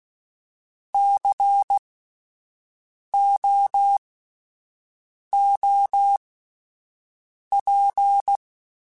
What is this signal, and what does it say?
Morse code, heard as a keyed tone, about 16 wpm, spaced as for 5 wpm, 800 Hz. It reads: COOP
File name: Alphabetical / Alphabetical mp3s / Cs @5